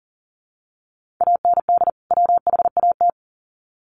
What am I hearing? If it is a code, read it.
Morse code, heard as a keyed tone, about 40 wpm, 700 Hz. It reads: AND WHAT
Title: Sets of 2 Words 40wpm